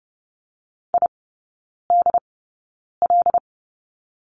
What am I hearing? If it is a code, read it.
Morse code, heard as a keyed tone, about 30 wpm, spaced as for 9 wpm, 700 Hz. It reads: IDL
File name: Lesson 15e - Number 5 - Three Characters 30wpm